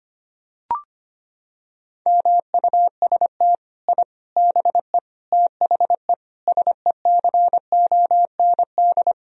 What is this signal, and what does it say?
Morse code, heard as a keyed tone, about 25 wpm, 700 Hz. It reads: MUST I BE THE SECOND
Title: Sentences from Top 1000 Words - Review Part 4 - 25wpm